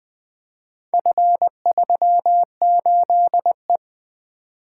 Morse code, heard as a keyed tone, about 20 wpm, 700 Hz. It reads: F38E
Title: Four Letter-Number Combinations - 20wpm